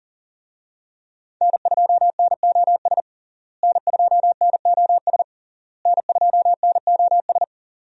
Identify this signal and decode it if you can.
Morse code, heard as a keyed tone, about 40 wpm, 700 Hz. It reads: N2NOS N2NOS N2NOS
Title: USA Call Signs - 5 Characters - Repeated 3x - 40wpm